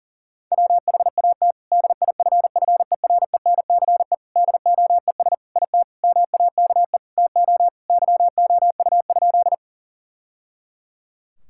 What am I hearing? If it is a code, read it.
Morse code, heard as a keyed tone, about 40 wpm, 700 Hz. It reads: WHAT DIFFERENCE DOES IT MAKE TO YOU?
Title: Sentences from Top 1000 Words - Review Part 5 - 40wpm